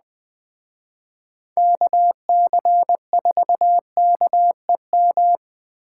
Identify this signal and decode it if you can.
Morse code, heard as a keyed tone, about 21 wpm, 700 Hz. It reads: KC4KEM